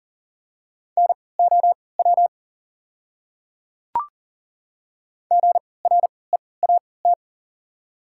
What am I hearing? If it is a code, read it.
Morse code, heard as a keyed tone, about 40 wpm, spaced as for 23 wpm, 700 Hz. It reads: NOW GREAT
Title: 3x Character Spacing - Top 500 Words - 40wpm